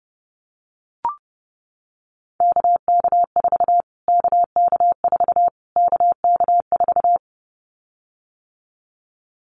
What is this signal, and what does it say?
Morse code, heard as a keyed tone, about 30 wpm, 700 Hz. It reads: KK4 KK4 KK4
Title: Callsign Prefixes for Top 25 DXCC - Repeated 3x - 30wpm